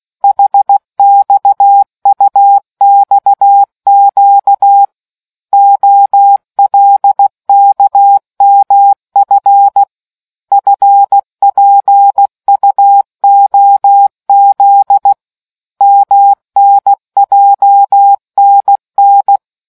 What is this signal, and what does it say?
Morse code, heard as a keyed tone, about 16 wpm, 800 Hz. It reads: HXUXQ OLKMF FPUOZ MNJNN